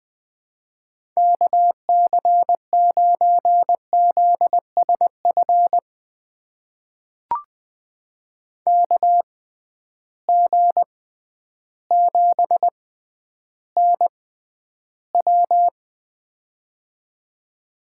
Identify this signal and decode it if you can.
Morse code, heard as a keyed tone, about 20 wpm, spaced as for 11 wpm, 700 Hz. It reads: KC9ZSF KG7NW
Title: USA Call Signs - 6x Character Spacing - 20wpm